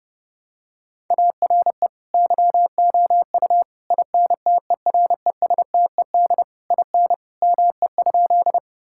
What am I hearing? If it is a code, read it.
Morse code, heard as a keyed tone, about 30 wpm, 700 Hz. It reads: ARE YOU INTERESTED IN ME?
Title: Sentences from Top 1000 Words - Review Part 6 - 30wpm